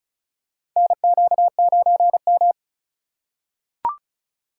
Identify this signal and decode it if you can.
Morse code, heard as a keyed tone, about 34 wpm, 700 Hz. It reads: NQ9M
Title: USA Call Signs - 4 Characters - 3x Character Spacing - 35wpm